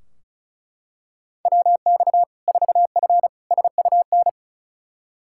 Morse code, heard as a keyed tone, about 36 wpm, 700 Hz. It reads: WX 4F SUN
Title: Practice QSO - Exchange 2 - Info - 35wpm